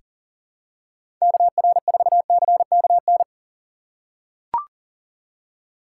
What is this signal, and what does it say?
Morse code, heard as a keyed tone, about 38 wpm, 700 Hz. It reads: KR4CKN
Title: USA Call Signs - 6 Characters - 6x Character Spacing - 40wpm